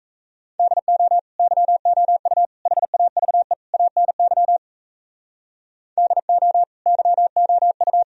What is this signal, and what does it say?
Morse code, heard as a keyed tone, about 35 wpm, 700 Hz. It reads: DO YOU HAVE ANY DO YOU
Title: Speed-Racing - Sets of 4 Words 28wpm